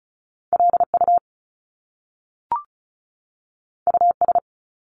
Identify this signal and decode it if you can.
Morse code, heard as a keyed tone, about 36 wpm, 700 Hz. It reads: LU US
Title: Top Two Letter Combinations in Words 35wpm